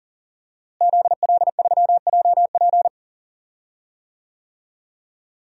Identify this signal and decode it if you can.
Morse code, heard as a keyed tone, about 37 wpm, 700 Hz. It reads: ZL3JP
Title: DXCC-100 Call Signs - 5 Characters - Repeated 2x - 40wpm